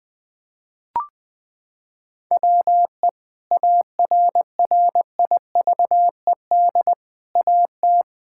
Morse code, heard as a keyed tone, about 20 wpm, 700 Hz. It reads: WE ARRIVED AT
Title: Sentences with 4-Word Phrases - 20wpm